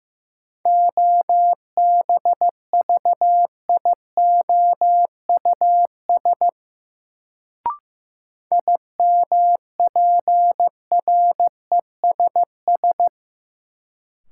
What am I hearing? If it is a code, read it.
Morse code, heard as a keyed tone, about 15 wpm, 700 Hz. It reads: OBVIOUS IMPRESS
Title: Sentences from Top 1600 Words - New Word Drill Alt - 15wpm